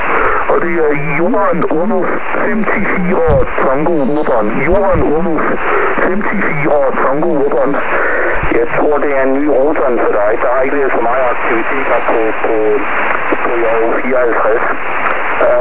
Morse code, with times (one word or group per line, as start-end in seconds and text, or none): none